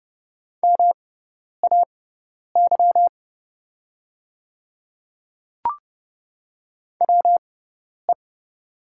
Morse code, heard as a keyed tone, about 30 wpm, spaced as for 10 wpm, 700 Hz. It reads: MAY WE
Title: 6x Character Spacing - Top 100 Words - 30wpm